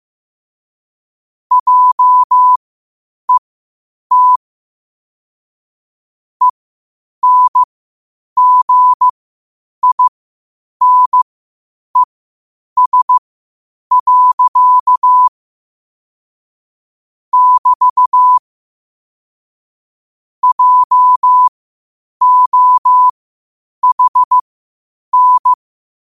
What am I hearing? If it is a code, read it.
Morse code, heard as a keyed tone, about 15 wpm, spaced as for 8 wpm, 1000 Hz. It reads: JET ENGINES. <BT> JOHN
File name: qotdfiles / 8wpm